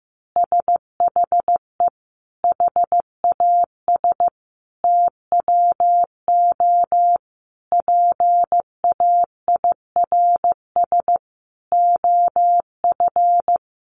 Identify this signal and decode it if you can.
Morse code, heard as a keyed tone, about 15 wpm, 700 Hz. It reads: SHE HAS TWO PAIRS OF
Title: Additional Easy Phrases 15wpm